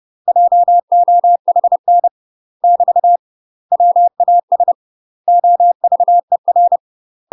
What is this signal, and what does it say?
Morse code, heard as a keyed tone, about 30 wpm, 700 Hz. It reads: JOHN <BT> WAS OVER
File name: UK MASTER QSOs TRAINING / 30WPM / UK MASTER QSOs 0001